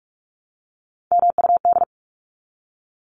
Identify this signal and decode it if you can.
Morse code, heard as a keyed tone, about 44 wpm, 700 Hz. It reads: MUD
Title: Top 1751-2000 Words 45wpm